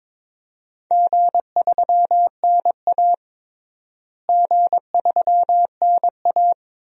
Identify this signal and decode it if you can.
Morse code, heard as a keyed tone, about 22 wpm, 700 Hz. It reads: G3NA G3NA
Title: DXCC-100 Call Signs - 4 Characters - Repeated 2x - 22wpm